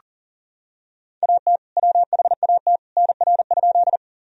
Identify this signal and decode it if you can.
Morse code, heard as a keyed tone, about 37 wpm, 700 Hz. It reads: AT WHAT NR?
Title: Sentences from Top 100 Words and Abbreviations 40wpm